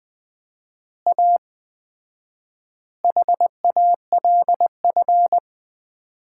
Morse code, heard as a keyed tone, about 20 wpm, 700 Hz. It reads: A HALF
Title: 4x Word-Spacing - Sets of 5 Words - 20wpm